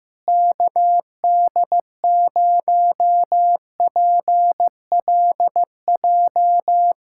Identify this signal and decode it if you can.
Morse code, heard as a keyed tone, about 15 wpm, 700 Hz. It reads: KD0PLJ
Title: USA Callsign DE Callsign - 6 Characters - 15wpm